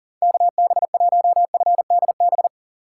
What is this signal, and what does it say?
Morse code, heard as a keyed tone, about 40 wpm, 700 Hz. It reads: KB1FDB
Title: USA Call Signs - 2x Character Spacing - 40wpm